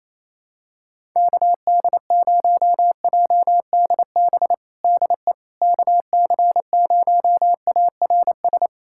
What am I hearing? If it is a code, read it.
Morse code, heard as a keyed tone, about 28 wpm, 700 Hz. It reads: KD0JDB DE KC0ARS
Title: USA Callsign DE Callsign - 6 Characters - 28wpm